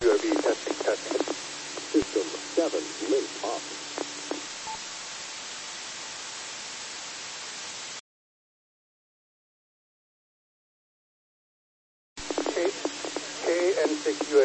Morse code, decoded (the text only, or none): none